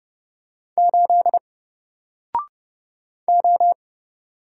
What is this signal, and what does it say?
Morse code, heard as a keyed tone, about 29 wpm, 700 Hz. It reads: 8 O
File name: Single-Letter-Number-30wpm